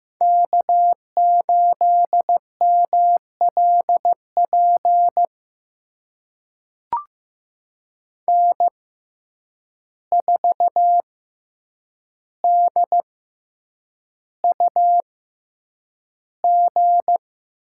Morse code, heard as a keyed tone, about 16 wpm, spaced as for 8 wpm, 700 Hz. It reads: K8MLP N4DUG
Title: USA Call Signs - 5 Characters - 6x Character Spacing - 15wpm